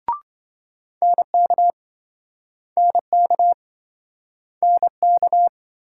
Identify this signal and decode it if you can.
Morse code, heard as a keyed tone, about 26 wpm, 700 Hz. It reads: NK NK NK